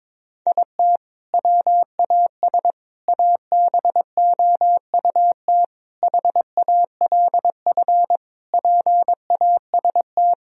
Morse code, heard as a keyed tone, about 22 wpm, 700 Hz. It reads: IT WAS ABOUT HALF PAST
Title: Sentences from Top 1000 Words - Review Part 3 - 22wpm